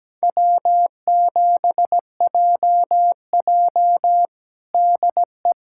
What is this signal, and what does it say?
Morse code, heard as a keyed tone, about 17 wpm, 700 Hz. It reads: W7JJ DE
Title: USA Callsign DE Callsign - 4 Characters - 17wpm